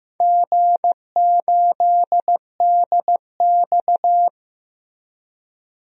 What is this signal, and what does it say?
Morse code, heard as a keyed tone, about 16 wpm, 700 Hz. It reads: G8DX
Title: DXCC-100 Call Signs - 4 Characters - Repeated 2x - 15wpm